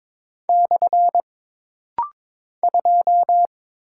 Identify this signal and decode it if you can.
Morse code, heard as a keyed tone, about 22 wpm, 700 Hz. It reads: / 2
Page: This rapid-fire timing is geared to push your ICR (Instant Character Recognition) skills.